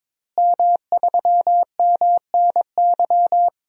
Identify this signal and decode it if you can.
Morse code, heard as a keyed tone, about 22 wpm, 700 Hz. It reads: M3MNY
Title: DXCC-100 Call Signs - 5 Characters - Repeated 3x - 22wpm